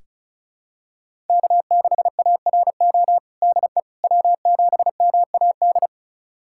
Morse code, heard as a keyed tone, about 34 wpm, 700 Hz. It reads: K6ARO DE W7MAD
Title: USA Callsign DE Callsign - 5 Characters - 35wpm